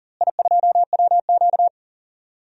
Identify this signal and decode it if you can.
Morse code, heard as a keyed tone, about 38 wpm, 700 Hz. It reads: I2WQ
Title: DXCC-100 Call Signs - 4 Characters - Repeated 2x - 40wpm